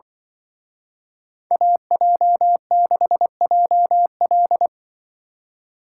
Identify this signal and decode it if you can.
Morse code, heard as a keyed tone, about 24 wpm, 700 Hz. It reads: AJ6JL
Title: Speed Racing - USA 2x2 Call Signs 20wpm